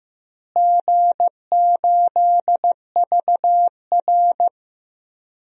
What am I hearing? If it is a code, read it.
Morse code, heard as a keyed tone, about 15 wpm, 700 Hz. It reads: G8VR
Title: DXCC-100 Call Signs - 4 Characters - Repeated 2x - 15wpm